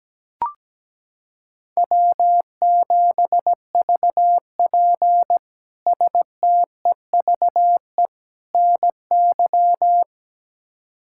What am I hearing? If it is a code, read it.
Morse code, heard as a keyed tone, about 17 wpm, 700 Hz. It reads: W7VP STEVE NY
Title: K1USN SST QSO Practice - 17wpm